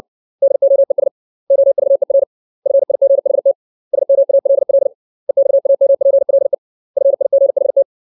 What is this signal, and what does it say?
Morse code, heard as a keyed tone, about 60 wpm, 550 Hz. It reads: DOES OVER FIGHT SMALL EXAMPLE FIGHT
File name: 632 words / 632 Words 60WPM